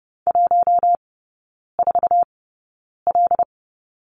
Morse code, heard as a keyed tone, about 30 wpm, spaced as for 9 wpm, 700 Hz. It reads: 14L
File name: Lesson 18e - Letter m - Three Characters 30wpm